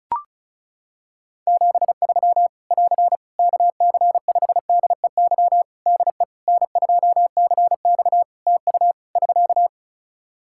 Morse code, heard as a keyed tone, about 36 wpm, 700 Hz. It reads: 73 <AR> KC5DEY DE N2CX TU <SK>
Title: Practice QSO - Exchange 4 - Ending - 35wpm